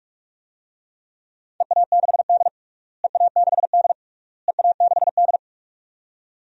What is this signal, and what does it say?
Morse code, heard as a keyed tone, about 46 wpm, 700 Hz. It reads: EA6D EA6D EA6D
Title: DXCC-100 Call Signs - 4 Characters - Repeated 3x - 45wpm